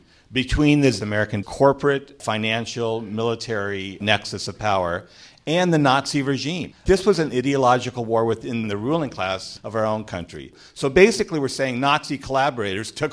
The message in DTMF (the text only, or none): none